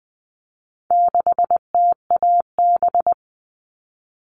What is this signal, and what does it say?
Morse code, heard as a keyed tone, about 20 wpm, 700 Hz. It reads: 6TAB
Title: All Cut Numbers - 4-Digit 20wpm